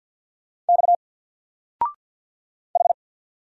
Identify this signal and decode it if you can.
Morse code, heard as a keyed tone, about 50 wpm, 700 Hz. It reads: X H
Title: Single Letter-Number 50wpm